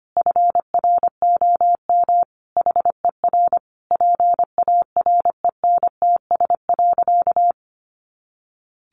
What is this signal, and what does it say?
Morse code, heard as a keyed tone, about 25 wpm, 700 Hz. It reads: FROM HER PARENTS.
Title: Easy Phrases 25wpm